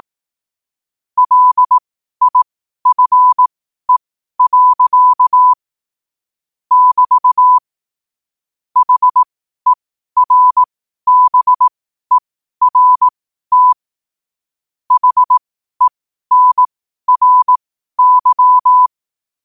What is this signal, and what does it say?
Morse code, heard as a keyed tone, about 18 wpm, spaced as for 12 wpm, 1000 Hz. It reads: LIFE. <BT> HERBERT HENRY
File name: qotdfiles / 12wpm